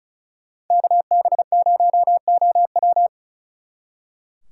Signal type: Morse code

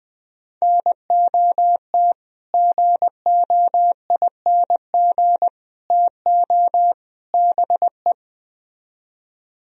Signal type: Morse code

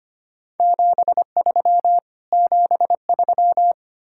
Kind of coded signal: Morse code

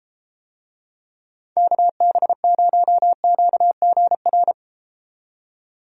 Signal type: Morse code

